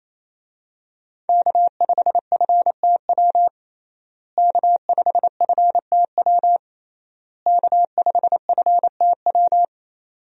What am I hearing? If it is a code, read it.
Morse code, heard as a keyed tone, about 28 wpm, 700 Hz. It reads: K5FTW K5FTW K5FTW
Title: USA Call Signs - 5 Characters - Repeated 3x - 28wpm